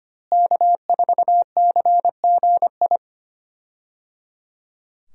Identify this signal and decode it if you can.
Morse code, heard as a keyed tone, about 25 wpm, 700 Hz. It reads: K4CGI
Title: USA Call Signs - 5 Characters - Repeated 3x - 25wpm